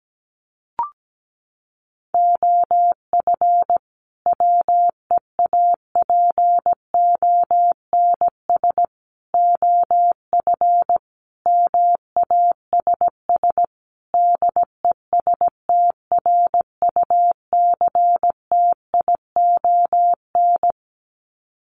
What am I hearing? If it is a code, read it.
Morse code, heard as a keyed tone, about 17 wpm, 700 Hz. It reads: OF WEAPONS OF MASS DESTRUCTION
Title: Sets of 5 Words Encore 17wpm